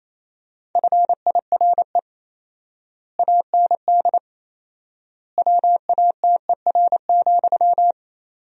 Morse code, heard as a keyed tone, about 28 wpm, 700 Hz. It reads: FIRE AND WATER,
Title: 4x Word-Spacing - Sentences from Top 400 Words - 28wpm